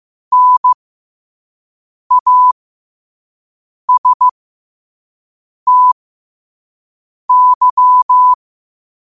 Morse code, heard as a keyed tone, about 15 wpm, spaced as for 5 wpm, 1000 Hz. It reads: NASTY